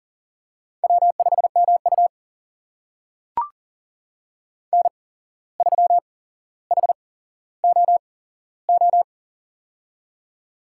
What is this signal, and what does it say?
Morse code, heard as a keyed tone, about 40 wpm, spaced as for 16 wpm, 700 Hz. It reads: W5MU N3HOO
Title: USA Call Signs - 8x Character Spacing - 40wpm